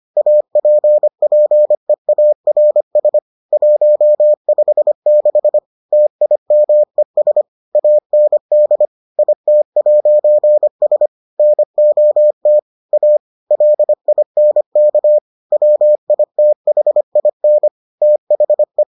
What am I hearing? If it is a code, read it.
Morse code, heard as a keyed tone, about 25 wpm, 600 Hz. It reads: APPEARS 156 TIMES AND IT'S NOT A LINK WITHIN THE